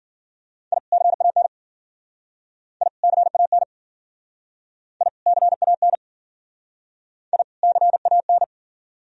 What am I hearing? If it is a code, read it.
Morse code, heard as a keyed tone, about 49 wpm, 700 Hz. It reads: I CAN I CAN I CAN I CAN
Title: Speed-Racing - Sets of 2 Words 40wpm